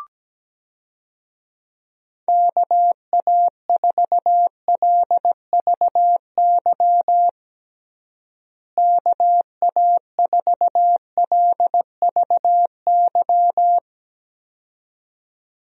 Morse code, heard as a keyed tone, about 17 wpm, 700 Hz. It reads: KA4LVY KA4LVY
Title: USA Call Signs - 6 Characters - Repeated 2x - 17wpm